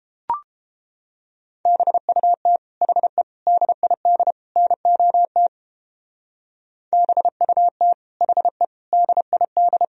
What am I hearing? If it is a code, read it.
Morse code, heard as a keyed tone, about 28 wpm, 700 Hz. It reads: BUT HE DID NOT BUT HE DID